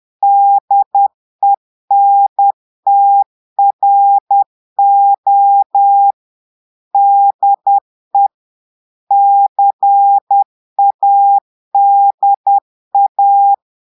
Morse code, heard as a keyed tone, about 10 wpm, 800 Hz. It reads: DENTRO DE CADA
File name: Portugais - texte en clair 1 - vitesse - 10 mots minute